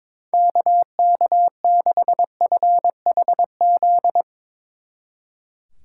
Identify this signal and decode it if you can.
Morse code, heard as a keyed tone, about 22 wpm, 700 Hz. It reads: KK6FHZ
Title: Speed Racing - USA 2x3 Call Signs 22wpm